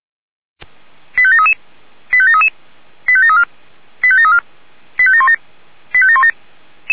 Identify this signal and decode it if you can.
Five-tone selective call (ZVEI), heard as quick runs of five tones, tones of about 70 ms. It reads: 85733 85733 85734 85734 85717 85717